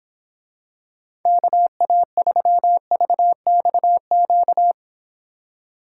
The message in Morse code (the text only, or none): KA3VXQ